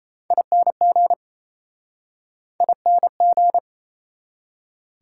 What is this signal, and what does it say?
Morse code, heard as a keyed tone, about 39 wpm, 700 Hz. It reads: ING ING